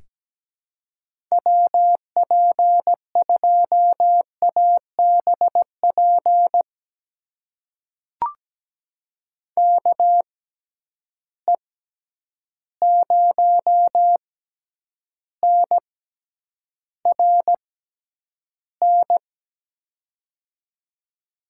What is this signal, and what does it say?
Morse code, heard as a keyed tone, about 17 wpm, spaced as for 9 wpm, 700 Hz. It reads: WP2ABP KE0NRN
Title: USA Call Signs - 6 Characters - 6x Character Spacing - 17wpm